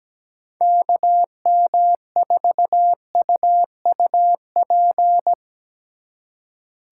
Morse code, heard as a keyed tone, about 17 wpm, 700 Hz. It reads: KM4UUP
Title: USA Call Signs - 3x Character Spacing - 17wpm